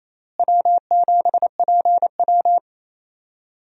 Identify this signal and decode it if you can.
Morse code, heard as a keyed tone, about 28 wpm, 700 Hz. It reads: W7PW
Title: USA Call Signs - 4 Characters - Repeated 3x - 28wpm